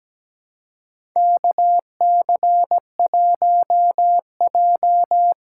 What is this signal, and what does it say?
Morse code, heard as a keyed tone, about 17 wpm, 700 Hz. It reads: KC1J
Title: USA Call Signs - 4 Characters - 3x Character Spacing - 17wpm